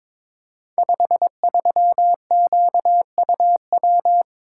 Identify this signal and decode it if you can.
Morse code, heard as a keyed tone, about 22 wpm, 700 Hz. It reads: 53QUW
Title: Code Groups - 5 Letter:Numbers - 22wpm